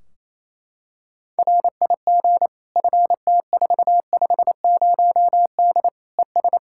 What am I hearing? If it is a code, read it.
Morse code, heard as a keyed tone, about 28 wpm, 700 Hz. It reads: RIG FT450D ES
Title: K1USN SST QSO Practice 28wpm